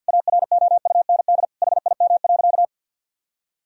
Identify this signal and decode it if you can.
Morse code, heard as a keyed tone, about 51 wpm, 700 Hz. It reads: AROUND HIM.
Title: Sentences from Top 1000 Words - Review Part 3 - 50wpm